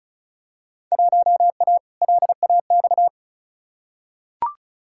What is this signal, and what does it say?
Morse code, heard as a keyed tone, about 36 wpm, 700 Hz. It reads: 1A LAX
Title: Field Day Exchanges - 35wpm